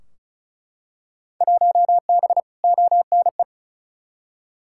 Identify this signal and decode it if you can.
Morse code, heard as a keyed tone, about 34 wpm, 700 Hz. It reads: 1B ONE
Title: Field Day Exchanges - 35wpm